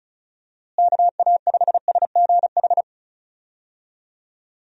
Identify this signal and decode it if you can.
Morse code, heard as a keyed tone, about 34 wpm, 700 Hz. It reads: KA5SGH
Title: USA Call Signs - 6 Characters - Repeated 2x - 35wpm